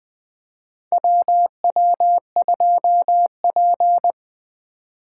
Morse code, heard as a keyed tone, about 21 wpm, 700 Hz. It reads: WW2P
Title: K1USN SST QSO Practice - 20wpm